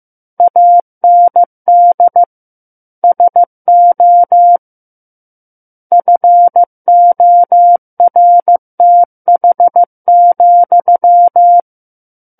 Morse code, heard as a keyed tone, about 15 wpm, 700 Hz. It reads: AND SO FORTH,
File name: Book One / 06-Chapter 6 - 15 WPM